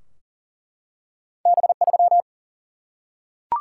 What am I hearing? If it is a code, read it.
Morse code, heard as a keyed tone, about 38 wpm, 700 Hz. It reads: B3
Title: Single-Letter Single-Number 40wpm